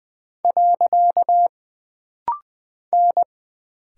Morse code, heard as a keyed tone, about 20 wpm, 700 Hz. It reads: . N
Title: Single Character (Letter:Number:Punctuation:Prosign) - Rapid-Fire - 20wpm